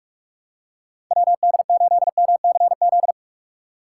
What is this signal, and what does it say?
Morse code, heard as a keyed tone, about 44 wpm, 700 Hz. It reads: WD8MCZ
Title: USA Call Signs - 6 Characters - 6x Character Spacing - 45wpm